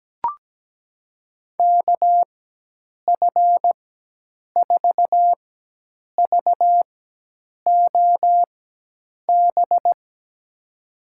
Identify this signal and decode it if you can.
Morse code, heard as a keyed tone, about 17 wpm, spaced as for 8 wpm, 700 Hz. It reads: KF4VOB
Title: USA Call Signs - 4x Character Spacing - 17wpm